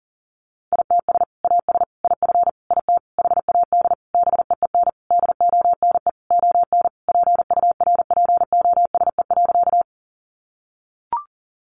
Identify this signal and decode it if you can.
Morse code, heard as a keyed tone, about 40 wpm, 700 Hz. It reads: ITS AS IF IT HAD BEEN DONE ON PURPOSE.
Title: Sentences from Top 800 Words 40wpm